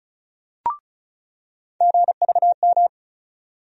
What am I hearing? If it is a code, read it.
Morse code, heard as a keyed tone, about 36 wpm, 700 Hz. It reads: GVM
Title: USA Call Sign Postfixes - 3 Letters 35wpm